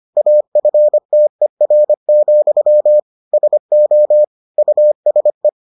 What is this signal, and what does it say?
Morse code, heard as a keyed tone, about 25 wpm, 600 Hz. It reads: AFTER, SO USE